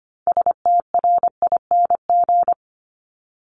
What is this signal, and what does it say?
Morse code, heard as a keyed tone, about 25 wpm, 700 Hz. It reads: STRING
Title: Multi-letter Phonemes 25wpm